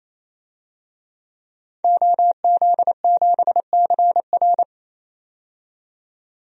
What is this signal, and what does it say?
Morse code, heard as a keyed tone, about 28 wpm, 700 Hz. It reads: OZ7CR
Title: DXCC-100 Call Signs - 5 Characters - 28wpm